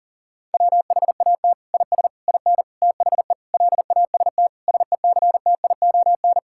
Morse code, heard as a keyed tone, about 37 wpm, 700 Hz. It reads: WHAT IS IN THE LAST SECTION
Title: Sentences from Top 1000 Words - Review Part 5 - 40wpm